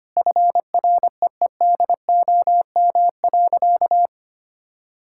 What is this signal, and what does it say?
Morse code, heard as a keyed tone, about 25 wpm, 700 Hz. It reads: FREEDOM.
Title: Sentences from Top 1000 Words - Review Part 5 - 25wpm